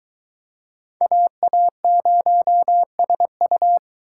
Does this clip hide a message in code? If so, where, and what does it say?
Yes: Morse code from 1.0 to 3.8 s, heard as a keyed tone, about 23 wpm, 700 Hz. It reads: AA0SU